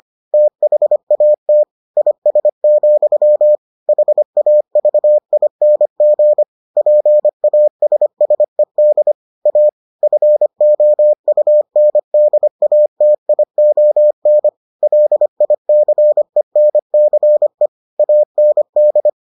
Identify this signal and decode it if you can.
Morse code, heard as a keyed tone, about 25 wpm, 600 Hz. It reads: THAT IS, HAVING PASSED A FOUNDATION LICENCE AND